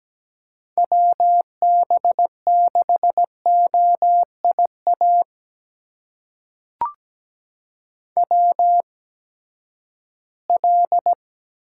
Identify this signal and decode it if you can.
Morse code, heard as a keyed tone, about 17 wpm, spaced as for 12 wpm, 700 Hz. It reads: WB6OIA WL
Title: USA Call Signs - 6 Characters - 8x Character Spacing - 17wpm